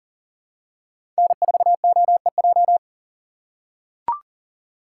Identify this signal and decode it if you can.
Morse code, heard as a keyed tone, about 37 wpm, 700 Hz. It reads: N4OEJ
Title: USA Call Signs - 5 Characters - 6x Character Spacing - 40wpm